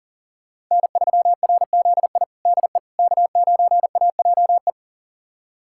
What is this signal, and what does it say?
Morse code, heard as a keyed tone, about 37 wpm, 700 Hz. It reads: N3RZI DE K9AJE
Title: USA Callsign DE Callsign - 5 Characters - 40wpm